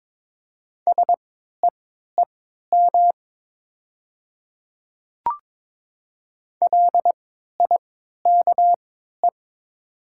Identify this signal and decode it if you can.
Morse code, heard as a keyed tone, about 22 wpm, spaced as for 12 wpm, 700 Hz. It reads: SEEM LIKE
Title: 3x Character Spacing - Top 300 Words - 22wpm